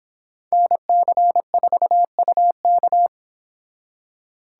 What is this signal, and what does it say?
Morse code, heard as a keyed tone, about 25 wpm, 700 Hz. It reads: NC4UK